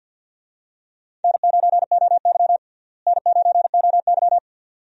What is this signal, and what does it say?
Morse code, heard as a keyed tone, about 50 wpm, 700 Hz. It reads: N9OY N9OY
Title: USA Call Signs - 4 Characters - Repeated 2x - 50wpm